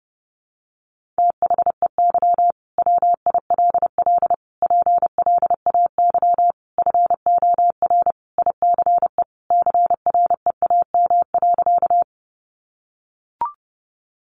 Morse code, heard as a keyed tone, about 30 wpm, 700 Hz. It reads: THEY WILL PLAY FOR ICE CREAM.
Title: Sentences from Top 1500 Words 30wpm